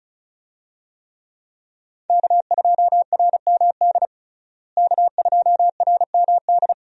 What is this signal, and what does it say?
Morse code, heard as a keyed tone, about 34 wpm, 700 Hz. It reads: K2RMD K2RMD
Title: USA Call Signs - 5 Characters - Repeated 2x - 35wpm